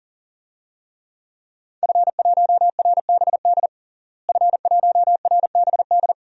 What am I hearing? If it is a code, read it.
Morse code, heard as a keyed tone, about 40 wpm, 700 Hz. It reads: F1RBD F1RBD
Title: DXCC-100 Call Signs - 5 Characters - Repeated 2x - 40wpm